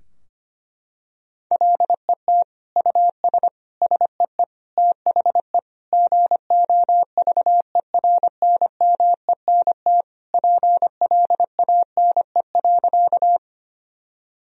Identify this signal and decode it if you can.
Morse code, heard as a keyed tone, about 25 wpm, 700 Hz. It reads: LET US SEE THE GOVERNMENT PLANE.
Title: Sentences from Top 1000 Words - Review Part 3 - 25wpm